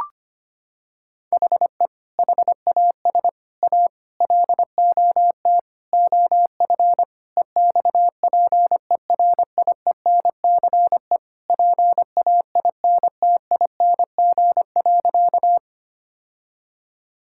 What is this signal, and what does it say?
Morse code, heard as a keyed tone, about 25 wpm, 700 Hz. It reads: HE HAS A LOT OF EXPERIENCE PAINTING.
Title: Sentences from Top 1000 Words - Review Part 6 - 25wpm